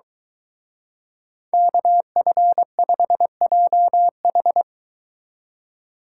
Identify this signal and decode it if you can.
Morse code, heard as a keyed tone, about 23 wpm, 700 Hz. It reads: KF5JH